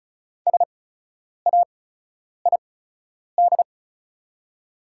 Morse code, heard as a keyed tone, about 36 wpm, spaced as for 10 wpm, 700 Hz. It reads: SAID
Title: 8x Character Spacing - Top 100 Words - 35wpm